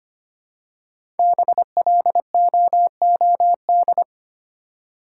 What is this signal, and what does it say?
Morse code, heard as a keyed tone, about 25 wpm, 700 Hz. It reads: BLOOD
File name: 5 Letter Words - First 3 Letters Given - 25wpm